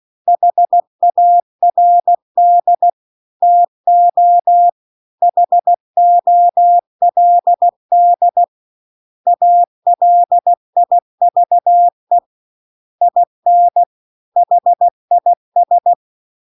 Morse code, heard as a keyed tone, about 16 wpm, 700 Hz. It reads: HARD TO HOLD ALIVE IN HIS
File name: CW-The-Big-Twoheaded-River-2-16WPM